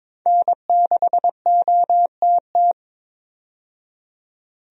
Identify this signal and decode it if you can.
Morse code, heard as a keyed tone, about 22 wpm, 700 Hz. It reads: N6OTT
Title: USA Call Signs - 5 Characters - 22wpm